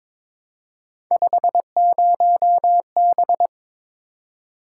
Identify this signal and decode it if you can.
Morse code, heard as a keyed tone, about 26 wpm, 700 Hz. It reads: 50B
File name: Speed-Racing - Three Letter-Number Combinations - 22wpm